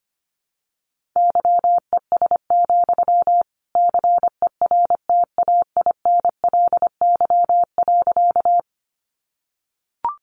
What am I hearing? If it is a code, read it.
Morse code, heard as a keyed tone, about 25 wpm, 700 Hz. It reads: YES, CERTAINLY.
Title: Sentences from Top 900 Words 25wpm